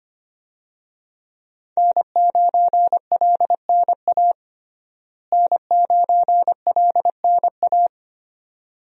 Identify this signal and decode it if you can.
Morse code, heard as a keyed tone, about 25 wpm, 700 Hz. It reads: N9LNA N9LNA
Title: USA Call Signs - 5 Characters - Repeated 2x - 25wpm